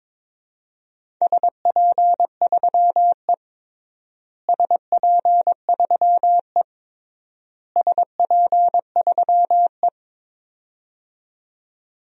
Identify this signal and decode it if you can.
Morse code, heard as a keyed tone, about 22 wpm, 700 Hz. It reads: SP3E SP3E SP3E
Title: DXCC-100 Call Signs - 4 Characters - Repeated 3x - 22wpm